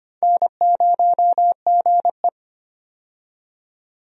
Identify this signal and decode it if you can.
Morse code, heard as a keyed tone, about 25 wpm, 700 Hz. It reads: N0GE
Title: USA Call Signs - 4 Characters - 25wpm